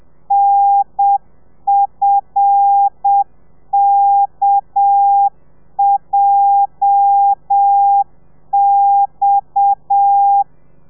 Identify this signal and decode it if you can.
Morse code, heard as a keyed tone, about 7 wpm, 800 Hz. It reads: NFKJX
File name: telegrafia / i2tfj / Lez8 30